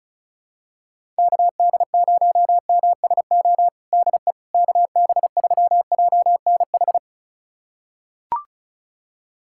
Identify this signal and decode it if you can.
Morse code, heard as a keyed tone, about 36 wpm, 700 Hz. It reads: KD0MSO DE KB3JNH
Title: USA Callsign DE Callsign - 6 Characters - 35wpm